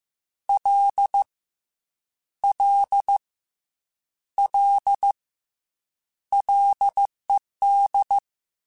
Morse code, heard as a keyed tone, about 15 wpm, 800 Hz. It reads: L L L LED